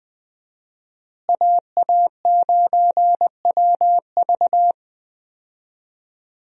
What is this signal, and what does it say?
Morse code, heard as a keyed tone, about 21 wpm, 700 Hz. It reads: AA9WV
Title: USA Call Signs - 5 Characters - Repeated 3x - 20wpm